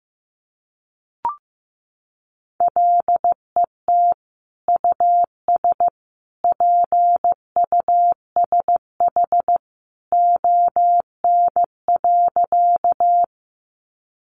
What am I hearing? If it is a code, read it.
Morse code, heard as a keyed tone, about 15 wpm, 700 Hz. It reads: LET US PUSH ON.
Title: Sentences from Top 1300 Words 15wpm